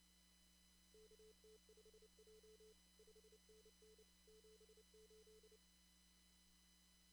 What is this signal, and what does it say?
Morse code, heard as a keyed tone, about 29 wpm, 440 Hz. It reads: KT5J 5NN 78
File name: CW / kt5j wpx cw-op-k5pi